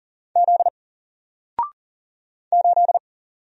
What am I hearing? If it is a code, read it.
Morse code, heard as a keyed tone, about 40 wpm, 700 Hz. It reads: Z 8
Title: Single Letter-Number 40wpm